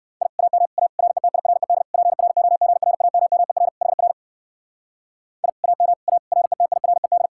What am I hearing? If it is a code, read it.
Morse code, heard as a keyed tone, about 58 wpm, 700 Hz. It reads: I AM A RETIRED PROGRAMMER HR I AM A RETIRED